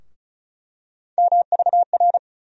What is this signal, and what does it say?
Morse code, heard as a keyed tone, about 36 wpm, 700 Hz. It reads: MVR
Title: USA Call Sign Postfixes - 3 Letters 35wpm